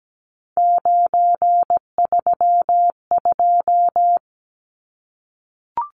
Morse code, heard as a keyed tone, about 17 wpm, 700 Hz. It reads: 932
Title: Numbers - 3-Digit 17wpm